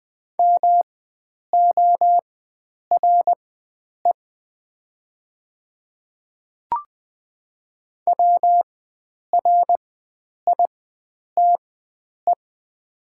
Morse code, heard as a keyed tone, about 20 wpm, spaced as for 9 wpm, 700 Hz. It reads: MORE WRITE
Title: 4x Character Spacing - Top 100 Words - 20wpm